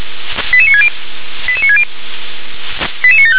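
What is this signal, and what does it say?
Five-tone selective call (ZVEI), heard as quick runs of five tones, tones of about 70 ms. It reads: 88960 88960 88975
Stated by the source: ZVEI five-tone selective call for address 88960, received by SDR.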